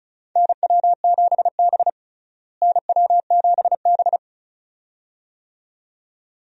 Morse code, heard as a keyed tone, about 34 wpm, 700 Hz. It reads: NW7B NW7B
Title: USA Call Signs - 4 Characters - Repeated 2x - 35wpm